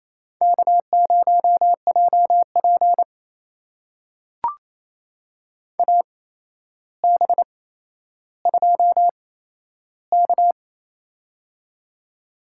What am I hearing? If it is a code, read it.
Morse code, heard as a keyed tone, about 28 wpm, spaced as for 12 wpm, 700 Hz. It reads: K0JP AB2K
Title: USA Call Signs - 4 Characters - 8x Character Spacing - 28wpm